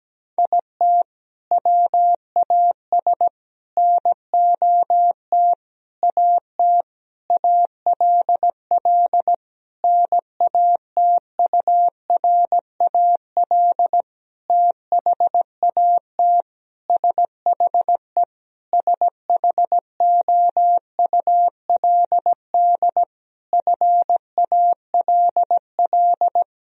Morse code, heard as a keyed tone, about 17 wpm, 700 Hz. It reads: IT WAS NOT AT ALL NATURAL THAT SHE SHOULD FALL
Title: Sentences from Top 1000 Words - Review Part 4 - 17wpm